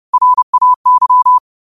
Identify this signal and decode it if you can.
Morse code, heard as a keyed tone, about 30 wpm, 1000 Hz. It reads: RAY